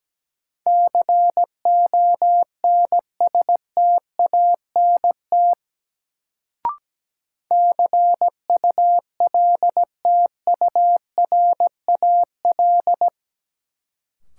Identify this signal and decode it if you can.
Morse code, heard as a keyed tone, about 17 wpm, 700 Hz. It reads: CONSTANT CULTURAL
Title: Sentences from Top 1500 Words - New Word Drill - 17wpm